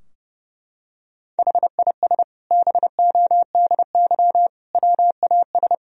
Morse code, heard as a keyed tone, about 32 wpm, 700 Hz. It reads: HIS BODY WAS
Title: Sentences from Top 1000 Words - Review Part 2 - 30wpm